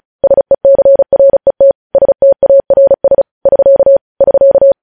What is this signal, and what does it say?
Morse code, heard as a keyed tone, about 36 wpm, 550 Hz. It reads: SECRET STARS <SK> <SK>
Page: news headlines in morse code at 35 wpm